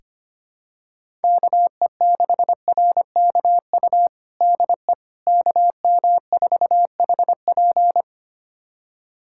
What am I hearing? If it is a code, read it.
Morse code, heard as a keyed tone, about 25 wpm, 700 Hz. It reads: KE6RKU DE KM4HP
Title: USA Callsign DE Callsign 25wpm